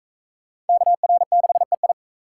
Morse code, heard as a keyed tone, about 37 wpm, 700 Hz. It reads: KR6EI